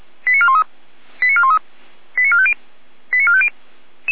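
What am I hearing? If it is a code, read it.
Five-tone selective call (ZVEI), heard as quick runs of five tones, tones of about 70 ms. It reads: 89413 89413 89450 89450
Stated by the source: ZVEI five-tone selective call received by SDR, fire brigade (Feuerwehr) Orlamünde.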